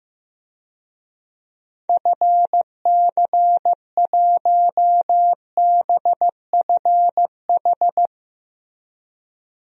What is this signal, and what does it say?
Morse code, heard as a keyed tone, about 15 wpm, 700 Hz. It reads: FC1BFH
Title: DXCC-100 Call Signs - 6 Characters - 15wpm